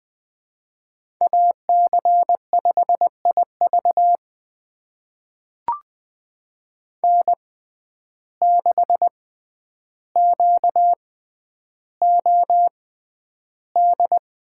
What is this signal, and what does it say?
Morse code, heard as a keyed tone, about 20 wpm, spaced as for 10 wpm, 700 Hz. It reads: AC5IV N6QOD
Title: USA Call Signs - 6x Character Spacing - 20wpm